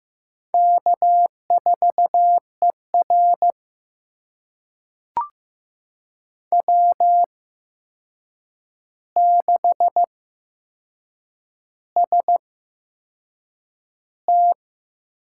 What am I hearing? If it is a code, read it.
Morse code, heard as a keyed tone, about 15 wpm, spaced as for 6 wpm, 700 Hz. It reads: K4ER W6ST
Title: USA Call Signs - 4 Characters - 8x Character Spacing - 15wpm